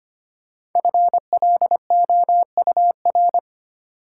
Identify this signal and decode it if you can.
Morse code, heard as a keyed tone, about 25 wpm, 700 Hz. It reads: FLOUR
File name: 5 Letter Words - First 3 Letters Given - 25wpm